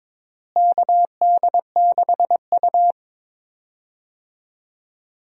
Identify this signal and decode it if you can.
Morse code, heard as a keyed tone, about 22 wpm, 700 Hz. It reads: KD6U
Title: USA Call Signs - 4 Characters - Repeated 3x - 22wpm